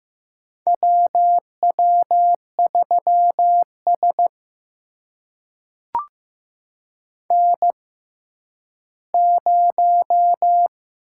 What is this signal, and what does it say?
Morse code, heard as a keyed tone, about 15 wpm, spaced as for 10 wpm, 700 Hz. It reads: WW3S N0
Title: USA Call Signs - 6x Character Spacing - 15wpm